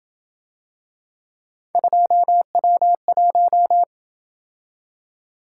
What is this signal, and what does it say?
Morse code, heard as a keyed tone, about 27 wpm, 700 Hz. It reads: 2W1